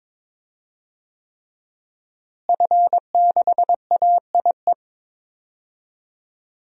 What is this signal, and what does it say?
Morse code, heard as a keyed tone, about 22 wpm, 700 Hz. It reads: F6AIE
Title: DXCC-100 Call Signs - 5 Characters - Repeated 3x - 22wpm